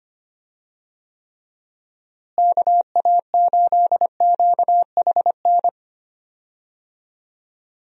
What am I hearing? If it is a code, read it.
Morse code, heard as a keyed tone, about 25 wpm, 700 Hz. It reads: KA8QHN
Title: USA Call Signs - 6 Characters - Repeated 3x - 25wpm